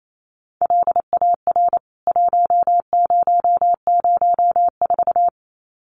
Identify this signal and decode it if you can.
Morse code, heard as a keyed tone, about 28 wpm, 700 Hz. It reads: LAR 1004
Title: CWT Exchange 28wpm